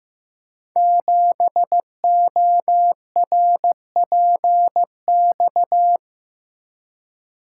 Morse code, heard as a keyed tone, about 15 wpm, 700 Hz. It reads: 7ORPX
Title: Code Groups - 5 Letter:Numbers - 15wpm